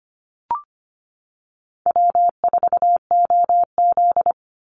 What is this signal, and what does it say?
Morse code, heard as a keyed tone, about 25 wpm, 700 Hz. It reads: W4OZ
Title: USA Call Signs 25wpm